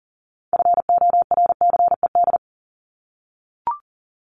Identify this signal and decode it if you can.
Morse code, heard as a keyed tone, about 40 wpm, 700 Hz. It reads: FORCED
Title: ED Suffix - Top 100 Words with Base Words 40wpm